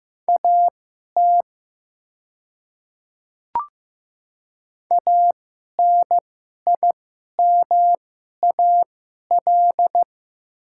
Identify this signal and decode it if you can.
Morse code, heard as a keyed tone, about 15 wpm, spaced as for 11 wpm, 700 Hz. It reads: AT ANIMAL
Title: 2x Character Spacing - Top 200 Words - 15wpm